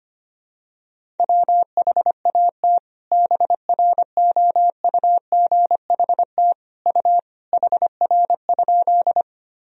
Morse code, heard as a keyed tone, about 25 wpm, 700 Hz. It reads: WHAT BROUGHT U HR?
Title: Sentences from Top 500 Words and Abbreviations 25wpm